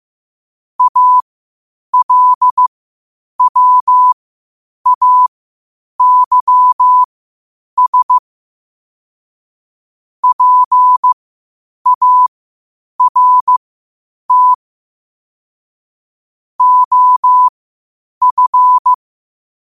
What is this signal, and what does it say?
Morse code, heard as a keyed tone, about 15 wpm, spaced as for 8 wpm, 1000 Hz. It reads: ALWAYS PART OF
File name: qotdfiles / 8wpm